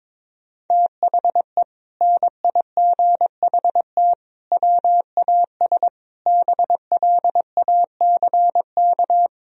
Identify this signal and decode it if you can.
Morse code, heard as a keyed tone, about 22 wpm, 700 Hz. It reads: THE NIGHT WAS BLACK
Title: Sentences from Top 1000 Words - Review Part 2 - 22wpm